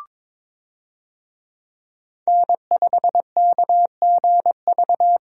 Morse code, heard as a keyed tone, about 22 wpm, 700 Hz. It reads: N5KGV
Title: USA Call Signs - 5 Characters - Repeated 2x - 22wpm